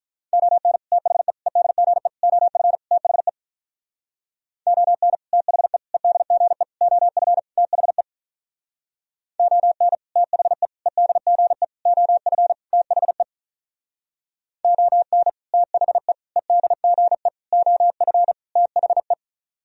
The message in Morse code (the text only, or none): ON THE EDGE OF THE ON THE EDGE OF THE ON THE EDGE OF THE ON THE EDGE OF THE